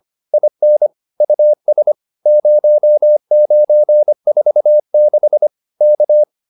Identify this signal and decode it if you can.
Morse code, heard as a keyed tone, about 25 wpm, 600 Hz. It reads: IN US 0946 K
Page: advanced-practice-files